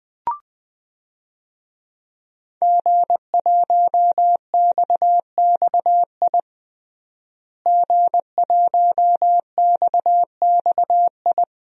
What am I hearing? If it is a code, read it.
Morse code, heard as a keyed tone, about 20 wpm, 700 Hz. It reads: G1XXI G1XXI
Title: DXCC-100 Call Signs - 5 Characters - Repeated 2x - 20wpm